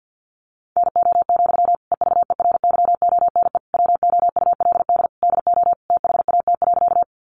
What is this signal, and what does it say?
Morse code, heard as a keyed tone, about 50 wpm, 700 Hz. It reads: NO, EVERYONE WOULD DO THAT.
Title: Sentences from Top 800 Words 50wpm